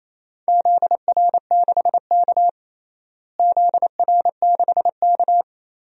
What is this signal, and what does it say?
Morse code, heard as a keyed tone, about 28 wpm, 700 Hz. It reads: ZR6K ZR6K
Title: DXCC-100 Call Signs - 4 Characters - Repeated 3x - 28wpm